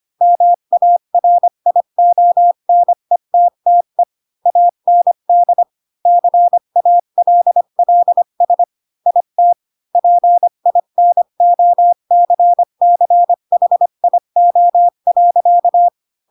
Morse code, heard as a keyed tone, about 25 wpm, 700 Hz. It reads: MARIONETTE AND CALLS IT PINOCCHIO.